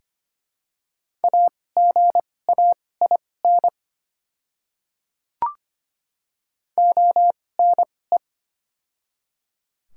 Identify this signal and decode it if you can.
Morse code, heard as a keyed tone, about 25 wpm, spaced as for 18 wpm, 700 Hz. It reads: AGAIN ONE
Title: 2x Character Spacing - Top 300 Words - 25wpm